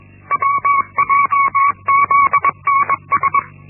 Morse code, heard as a keyed tone, about 21 wpm, 1120 Hz. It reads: WJZNS
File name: Linksmakalnis-1